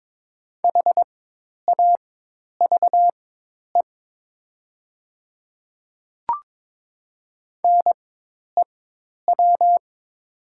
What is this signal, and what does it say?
Morse code, heard as a keyed tone, about 22 wpm, spaced as for 10 wpm, 700 Hz. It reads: HAVE NEW
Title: 4x Character Spacing - Top 200 Words - 22wpm